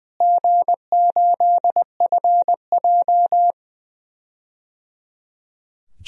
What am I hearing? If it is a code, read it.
Morse code, heard as a keyed tone, about 21 wpm, 700 Hz. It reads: G8FJ
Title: DXCC-100 Call Signs - 4 Characters - Repeated 3x - 20wpm